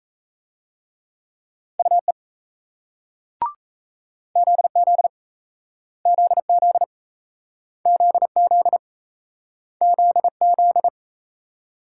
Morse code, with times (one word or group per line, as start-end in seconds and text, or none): 1.8-2.1 UE
4.4-5.1 ZZ
6.1-6.8 ZZ
7.9-8.8 ZZ
9.8-10.9 ZZ